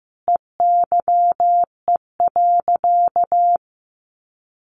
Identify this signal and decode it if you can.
Morse code, heard as a keyed tone, about 15 wpm, 700 Hz. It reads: EYE.
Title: Sentences from Top 1400 Words 15wpm